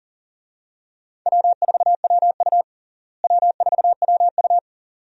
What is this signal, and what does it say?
Morse code, heard as a keyed tone, about 40 wpm, 700 Hz. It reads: W4WU W4WU
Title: USA Call Signs - 4 Characters - Repeated 2x - 40wpm